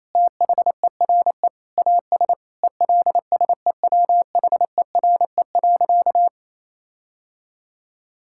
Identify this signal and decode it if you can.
Morse code, heard as a keyed tone, about 28 wpm, 700 Hz. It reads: THERE AS ELSEWHERE.
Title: Sentences from Top 1600 Words 28wpm